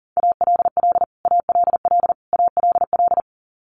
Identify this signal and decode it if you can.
Morse code, heard as a keyed tone, about 40 wpm, 700 Hz. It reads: ALL ALL ALL
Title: Top 300 Words Repeated 40wpm